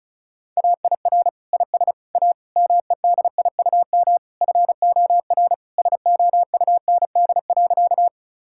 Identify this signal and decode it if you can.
Morse code, heard as a keyed tone, about 34 wpm, 700 Hz. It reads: AIR IS A MEDIUM FOR SOUND.
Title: Sentences from Top 1000 Words - Review Part 6 - 35wpm